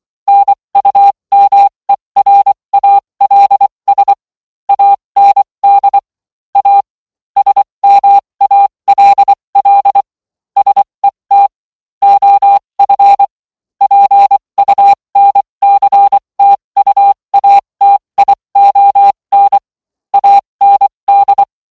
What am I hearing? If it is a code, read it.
Morse code, heard as a keyed tone, about 24 wpm, spaced as for 20 wpm, 800 Hz. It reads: NUMERALS AND A SMALL SET OF PUNCTUATION AND